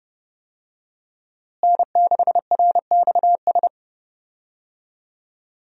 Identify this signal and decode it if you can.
Morse code, heard as a keyed tone, about 30 wpm, 700 Hz. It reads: N6RXS
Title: USA Call Signs - 5 Characters - Repeated 3x - 30wpm